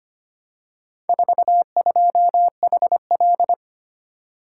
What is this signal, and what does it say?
Morse code, heard as a keyed tone, about 25 wpm, 700 Hz. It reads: 42HL
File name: practice / 25wpm / Four Letter-Number Combinations - 25wpm